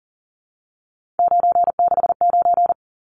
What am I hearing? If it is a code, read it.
Morse code, heard as a keyed tone, about 40 wpm, 700 Hz. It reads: 969
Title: Numbers - 3-Digit 40wpm